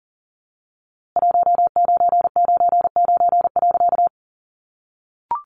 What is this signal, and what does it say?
Morse code, heard as a keyed tone, about 40 wpm, 700 Hz. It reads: 1999.
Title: A Sound Of Thunder 40wpm